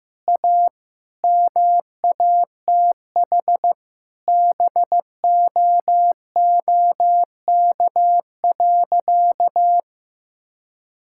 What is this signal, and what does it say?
Morse code, heard as a keyed tone, about 16 wpm, 700 Hz. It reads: A MATH BOOK.
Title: Sentences from Top 1000 Words - Review Part 6 - 15wpm